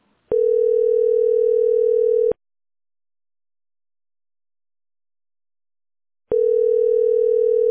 Ringback tone (North America), 440 + 480 Hz.